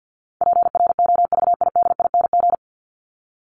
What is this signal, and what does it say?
Morse code, heard as a keyed tone, about 50 wpm, 700 Hz. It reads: PROVIDING
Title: ING Suffix - Top 100 Words with Base Words 50wpm